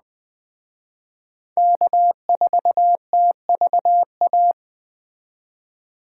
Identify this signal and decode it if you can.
Morse code, heard as a keyed tone, about 19 wpm, 700 Hz. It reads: K4TVA